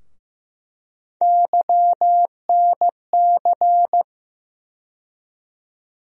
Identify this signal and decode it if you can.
Morse code, heard as a keyed tone, about 15 wpm, 700 Hz. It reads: YNC
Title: USA Call Sign Postfixes - 3 Letters 15wpm